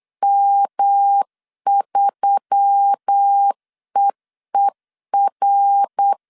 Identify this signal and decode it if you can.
Morse code, heard as a keyed tone, about 8 wpm, 800 Hz. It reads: M3EER
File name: CW / DET / 1 9 1500C S A 8wpm 5637